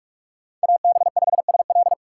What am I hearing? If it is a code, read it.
Morse code, heard as a keyed tone, about 47 wpm, 700 Hz. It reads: AB5SL
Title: USA Call Signs - 5 Characters - 4x Character Spacing - 45wpm